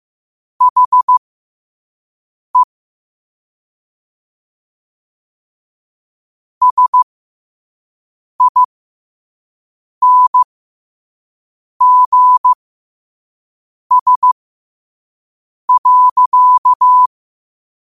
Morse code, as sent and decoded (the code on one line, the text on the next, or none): .... / ... .. -. --. ... .-.-.-
HE SINGS.